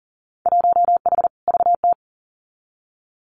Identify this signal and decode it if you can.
Morse code, heard as a keyed tone, about 42 wpm, 700 Hz. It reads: 1H VT
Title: Winter Field Day Exchanges Encore 40wpm